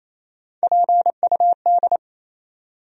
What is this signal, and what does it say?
Morse code, heard as a keyed tone, about 27 wpm, 700 Hz. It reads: PUD
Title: USA Call Sign Postfixes - 3 Letters 28wpm